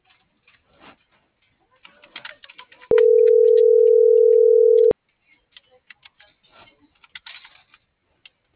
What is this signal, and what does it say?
Ringback tone (North America), 440 + 480 Hz.